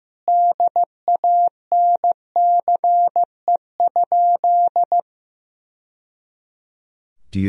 Morse code, heard as a keyed tone, about 15 wpm, 700 Hz. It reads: DANCE?
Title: Sentences from Top 1000 Words - Review Part 4 - 15wpm